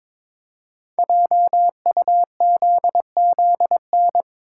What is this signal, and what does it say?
Morse code, heard as a keyed tone, about 22 wpm, 700 Hz. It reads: JUZZN
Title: Code Groups - 5 Letters - 22wpm